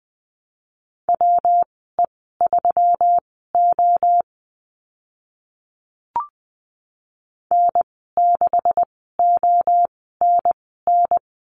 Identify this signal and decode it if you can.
Morse code, heard as a keyed tone, about 20 wpm, spaced as for 14 wpm, 700 Hz. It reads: WE3O N6ONN
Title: 140 Lesson 23g - Letter f - Call Signs 20wpm